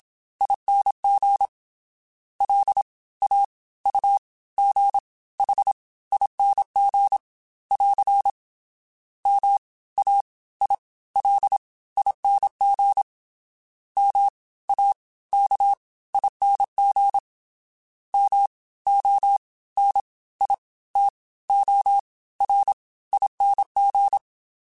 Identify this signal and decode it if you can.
Morse code, heard as a keyed tone, about 27 wpm, spaced as for 18 wpm, 800 Hz. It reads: ING LAUGHING <AR> MAILING MAKING MONITORING